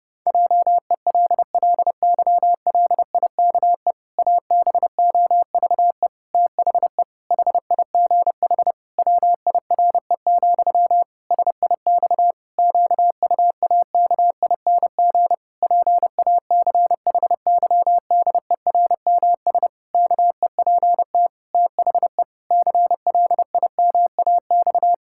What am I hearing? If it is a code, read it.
Morse code, heard as a keyed tone, about 32 wpm, 700 Hz. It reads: JELLYLIKE ABOVE THE HIGH WIRE, SIX QUAKING PACHYDERMS KEPT THE CLIMAX
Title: Pangram Challenge v2 - 30wpm